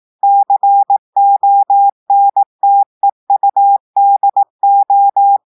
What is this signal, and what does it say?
Morse code, heard as a keyed tone, about 18 wpm, 800 Hz. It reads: CONTEUDO
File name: portugais texte en clair 2 vitesse 18 mots minutes